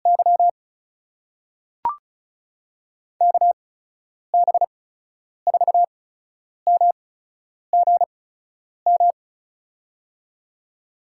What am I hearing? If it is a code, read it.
Morse code, heard as a keyed tone, about 36 wpm, spaced as for 10 wpm, 700 Hz. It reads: Y KB4MGM
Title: USA Call Signs - 6 Characters - 8x Character Spacing - 35wpm